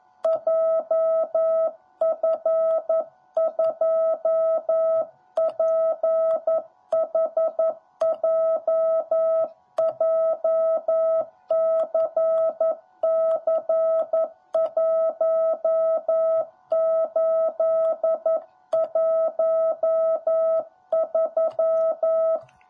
Morse code, heard as a keyed tone, about 11 wpm, 650 Hz. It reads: JF2PHJ JCC1813
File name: jh8cbh / 2021cw / cw10b